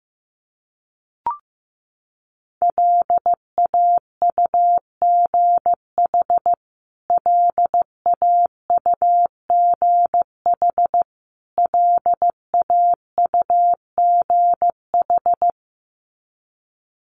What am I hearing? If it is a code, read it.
Morse code, heard as a keyed tone, about 15 wpm, 700 Hz. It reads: LAUGH LAUGH LAUGH
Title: Top 500 Words Repeated 15wpm